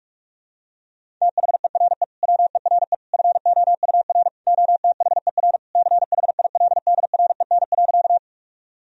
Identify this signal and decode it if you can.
Morse code, heard as a keyed tone, about 43 wpm, 700 Hz. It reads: THERE WERE FOUR OTHER CHILDREN.
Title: Sentences from Top 1000 Words - Review Part 1 - 45wpm